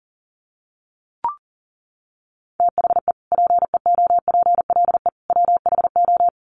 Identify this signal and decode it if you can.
Morse code, heard as a keyed tone, about 40 wpm, 700 Hz. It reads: THE PEOPLE WHO
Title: Sets of 3 Words 40wpm